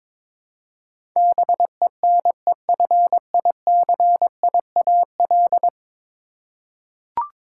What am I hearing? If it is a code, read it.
Morse code, heard as a keyed tone, about 22 wpm, 700 Hz. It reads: BENEFICIAL
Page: morse-code-ninja-practice-22wpm